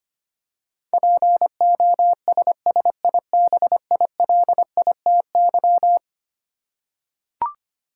Morse code, heard as a keyed tone, about 25 wpm, 700 Hz. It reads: POSSIBILITY
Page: morse-code-ninja-practice-25wpm